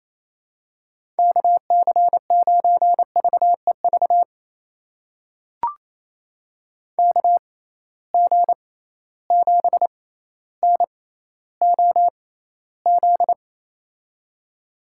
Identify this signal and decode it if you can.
Morse code, heard as a keyed tone, about 28 wpm, spaced as for 14 wpm, 700 Hz. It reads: KC9VEV KG7NOZ
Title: USA Call Signs - 6 Characters - 6x Character Spacing - 28wpm